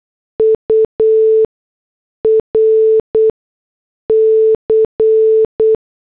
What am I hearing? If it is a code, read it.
Morse code, heard as a keyed tone, about 8 wpm, spaced as for 6 wpm, 440 Hz. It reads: URC